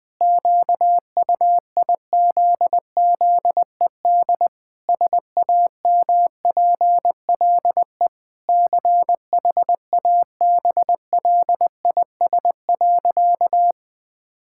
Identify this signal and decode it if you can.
Morse code, heard as a keyed tone, about 20 wpm, 700 Hz. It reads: QUIZZED SAMPLE CHABLIS.
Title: Pangram Challenge v2 - 20wpm